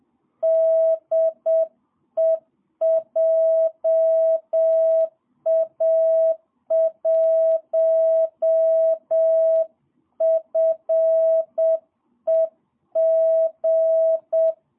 Morse code, heard as a keyed tone, about 7 wpm, 650 Hz. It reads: DE JA1FEG
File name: jh8cbh / 2022cw